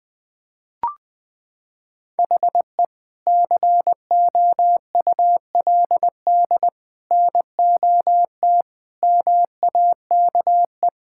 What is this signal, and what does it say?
Morse code, heard as a keyed tone, about 20 wpm, 700 Hz. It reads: HE COULD NOT MAKE
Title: Sentences from Top 1600 Words 20wpm